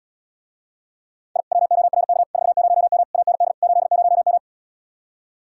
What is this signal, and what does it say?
Morse code, heard as a keyed tone, about 79 wpm, 700 Hz. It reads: I WORK 40M AND 80M